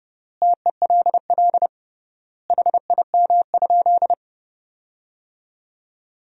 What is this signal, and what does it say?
Morse code, heard as a keyed tone, about 30 wpm, 700 Hz. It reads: TELL HIM?
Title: 3x Word-Spacing - Sentences from Top 200 Words - 30wpm